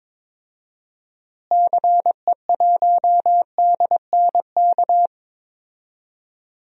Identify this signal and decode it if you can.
Morse code, heard as a keyed tone, about 21 wpm, 700 Hz. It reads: CE1DNK
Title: DXCC-100 Call Signs - 6 Characters - Repeated 2x - 22wpm